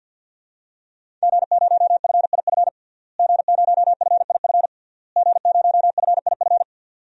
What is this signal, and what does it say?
Morse code, heard as a keyed tone, about 50 wpm, 700 Hz. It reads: G0FIF G0FIF G0FIF
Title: DXCC-100 Call Signs - 5 Characters - Repeated 3x - 50wpm